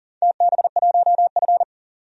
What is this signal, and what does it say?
Morse code, heard as a keyed tone, about 40 wpm, 700 Hz. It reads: TB1F
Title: DXCC-100 Call Signs - 4 Characters - 40wpm